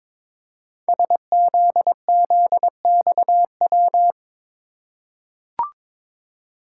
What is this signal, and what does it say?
Morse code, heard as a keyed tone, about 22 wpm, 700 Hz. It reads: SZZXW
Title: Code Groups - 5 Letters - 22wpm